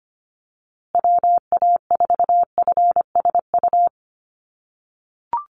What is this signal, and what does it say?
Morse code, heard as a keyed tone, about 25 wpm, 700 Hz. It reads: WA4FSU
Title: USA Call Signs Repeated 2x 25wpm